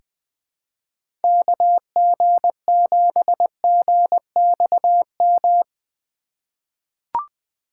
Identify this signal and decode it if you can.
Morse code, heard as a keyed tone, about 20 wpm, 700 Hz. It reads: KG7GXM
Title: USA Call Signs - 6 Characters - 3x Character Spacing - 20wpm